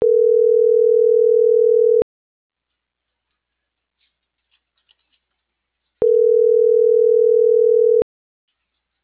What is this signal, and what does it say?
Ringback tone (North America), 440 + 480 Hz.